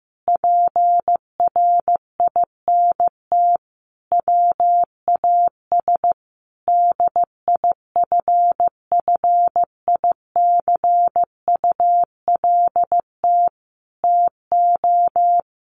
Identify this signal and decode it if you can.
Morse code, heard as a keyed tone, about 15 wpm, 700 Hz. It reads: PRINT WAS DIFFICULT TO
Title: Sentences from Top 1400 Words 15wpm